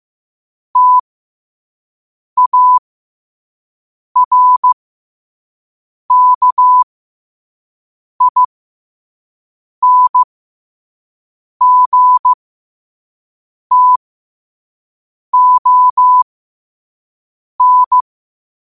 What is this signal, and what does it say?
Morse code, heard as a keyed tone, about 15 wpm, spaced as for 5 wpm, 1000 Hz. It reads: TARKINGTON
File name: qotdfiles / 5wpm